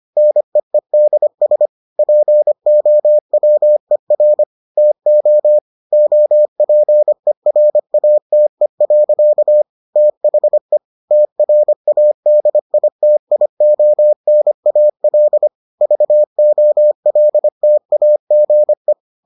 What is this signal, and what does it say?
Morse code, heard as a keyed tone, about 25 wpm, 600 Hz. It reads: NEEDS POWER TO OPERATE. THE TRADITIONAL VOLTAGE